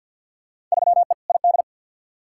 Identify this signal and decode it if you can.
Morse code, heard as a keyed tone, about 52 wpm, 700 Hz. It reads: 3E ID
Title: Field Day Exchanges Encore - 50wpm